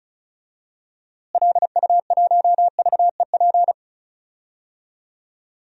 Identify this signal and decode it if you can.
Morse code, heard as a keyed tone, about 36 wpm, 700 Hz. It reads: LU1VEP
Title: DXCC-100 Call Signs - 6 Characters - Repeated 2x - 35wpm